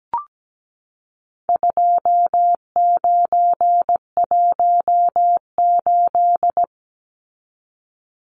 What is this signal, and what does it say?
Morse code, heard as a keyed tone, about 17 wpm, 700 Hz. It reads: 2918
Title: Numbers - 4-Digit 17wpm